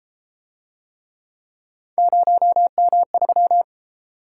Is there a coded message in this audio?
Yes: Morse code from 2.0 to 3.6 s, heard as a keyed tone, about 33 wpm, 700 Hz. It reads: 0M3